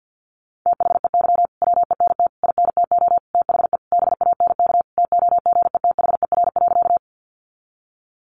Morse code, heard as a keyed tone, about 50 wpm, 700 Hz. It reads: THEY WENT INTO THE BANK TOGETHER.
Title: Sentences from Top 900 Words 50wpm